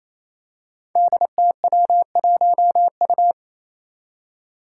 Morse code, heard as a keyed tone, about 29 wpm, 700 Hz. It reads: DTW1U
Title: Five Challenging Letter-Number Combinations 28wpm